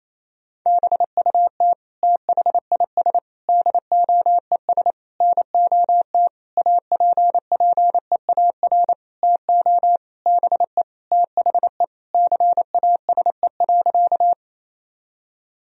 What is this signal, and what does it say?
Morse code, heard as a keyed tone, about 28 wpm, 700 Hz. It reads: BUT THIS DOES NOT APPEAR TO BE THE CASE.
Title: Sentences from Top 1000 Words - Review Part 2 - 28wpm